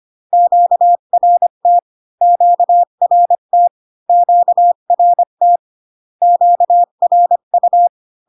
Morse code, heard as a keyed tone, about 25 wpm, 700 Hz. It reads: QRT QRT QRT QRU